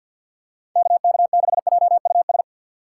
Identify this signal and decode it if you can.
Morse code, heard as a keyed tone, about 50 wpm, 700 Hz. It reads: KK6JUS
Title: USA Call Signs - 6 Characters - 3x Character Spacing - 50wpm